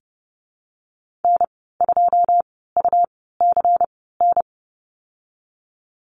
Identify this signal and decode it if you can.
Morse code, heard as a keyed tone, about 32 wpm, 700 Hz. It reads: N 2 U C N
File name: Lesson 19g - Letter w - Call Signs 30wpm